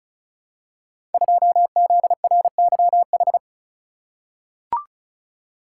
Morse code, heard as a keyed tone, about 36 wpm, 700 Hz. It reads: 2ZRYH
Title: Code Groups - 5 Letter:Numbers - 35wpm